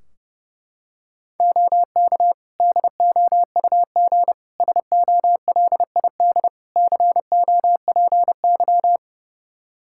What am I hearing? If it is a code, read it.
Morse code, heard as a keyed tone, about 30 wpm, 700 Hz. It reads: OK DOUG SOLID COPY
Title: K1USN SST QSO Practice 30wpm